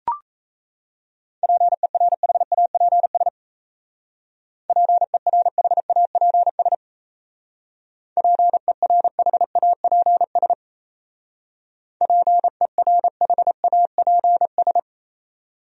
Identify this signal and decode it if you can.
Morse code, heard as a keyed tone, about 35 wpm, 700 Hz. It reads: PERHAPS PERHAPS PERHAPS PERHAPS
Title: Speed-Racing Top 500 Words 28wpm